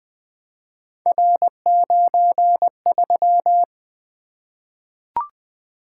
Morse code, heard as a keyed tone, about 20 wpm, 700 Hz. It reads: R93
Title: Three Letter-Number Combinations - 20wpm